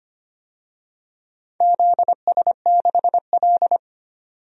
Morse code, heard as a keyed tone, about 25 wpm, 700 Hz. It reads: ZS6L
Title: DXCC-100 Call Signs - 4 Characters - 25wpm